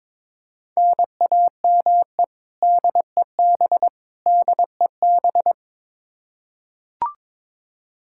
Morse code, heard as a keyed tone, about 21 wpm, 700 Hz. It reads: NAME DEB DEB
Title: Practice QSO - Exchange 1 - Beginning - 22wpm